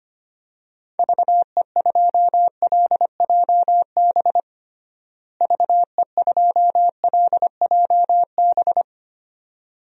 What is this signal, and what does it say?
Morse code, heard as a keyed tone, about 25 wpm, 700 Hz. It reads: VE2LJB VE2LJB
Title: DXCC-100 Call Signs - 6 Characters - Repeated 2x - 25wpm